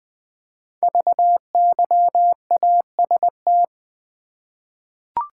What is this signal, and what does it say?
Morse code, heard as a keyed tone, about 20 wpm, 700 Hz. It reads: VYAST
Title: Code Groups - 5 Letters - 20wpm